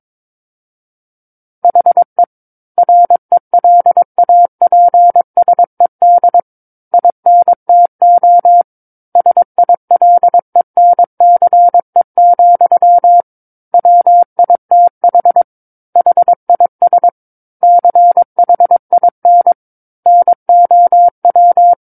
Morse code, heard as a keyed tone, about 22 wpm, 700 Hz. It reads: HE RELAPSED INTO SILENCE, WITH HIS CHIN NOW
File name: Book One / 13-Chapter 13 - 22 WPM